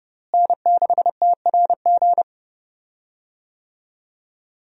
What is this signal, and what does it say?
Morse code, heard as a keyed tone, about 29 wpm, 700 Hz. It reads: N6TRG
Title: USA Call Signs - 5 Characters - Repeated 3x - 30wpm